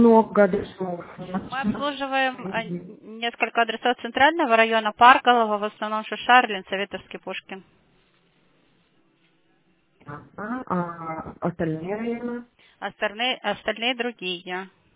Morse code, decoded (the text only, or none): none